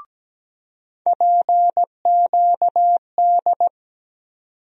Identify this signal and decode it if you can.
Morse code, heard as a keyed tone, about 17 wpm, 700 Hz. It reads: PQD
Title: USA Call Sign Postfixes - 3 Letters 17wpm